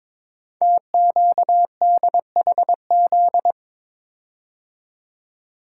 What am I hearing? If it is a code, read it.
Morse code, heard as a keyed tone, about 22 wpm, 700 Hz. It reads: TQDHZ
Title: Code Groups - 5 Letter:Numbers - 22wpm